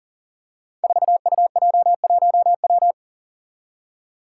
Morse code, heard as a keyed tone, about 40 wpm, 700 Hz. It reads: 4UJ1W
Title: Five Challenging Letter-Number Combinations 40wpm